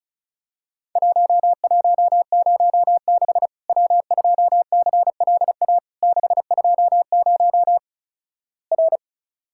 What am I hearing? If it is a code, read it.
Morse code, heard as a keyed tone, about 36 wpm, 700 Hz. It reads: 1106 W2CLA 620 R
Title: Worked All Europe DX Contest - QTC Exchange - 35wpm